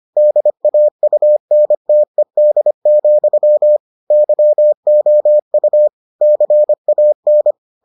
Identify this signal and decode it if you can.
Morse code, heard as a keyed tone, about 25 wpm, 600 Hz. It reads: DAUNTED, YOU CAN